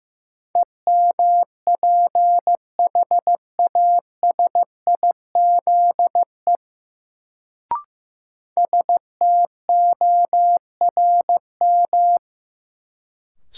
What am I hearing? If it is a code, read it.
Morse code, heard as a keyed tone, about 15 wpm, 700 Hz. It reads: EMPHASIZE STORM
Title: Sentences from Top 1500 Words - New Word Drill Alt - 15wpm